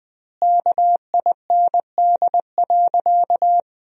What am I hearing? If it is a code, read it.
Morse code, heard as a keyed tone, about 20 wpm, 700 Hz. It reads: KIND.
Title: 2x Word-Spacing - Sentences from Top 500 Words - 20wpm